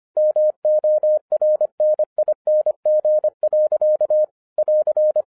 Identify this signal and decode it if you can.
Morse code, heard as a keyed tone, about 25 wpm, 610 Hz. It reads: MORNING. <AR>
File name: 2011-07-mp3 / n44 JOHN K3TN 25wpm